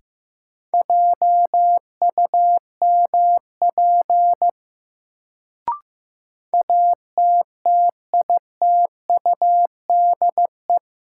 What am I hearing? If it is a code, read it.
Morse code, heard as a keyed tone, about 15 wpm, 700 Hz. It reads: JUMP ATTITUDE
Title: Sentences from Top 1300 Words - New Word Drill Alt - 15wpm